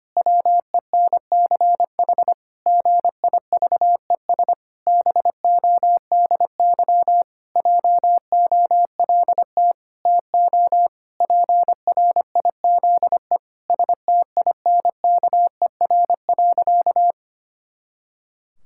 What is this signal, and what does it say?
Morse code, heard as a keyed tone, about 25 wpm, 700 Hz. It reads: WENCH GIVES BODY JOLT TO PRIZE STINKER.
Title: Pangram Challenge v2 - 25wpm